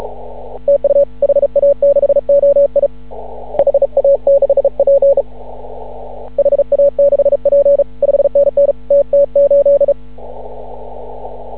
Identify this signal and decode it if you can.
Morse code, heard as a keyed tone, about 36 wpm, 560 Hz. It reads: TU HA6OI HA6P HA6P 5NN TT8
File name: CW / ha6oi 2